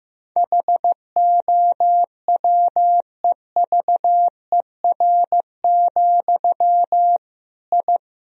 Morse code, heard as a keyed tone, about 16 wpm, 700 Hz. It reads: HOWEVER, I
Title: Sentences from Top 1000 Words - Review Part 3 - 15wpm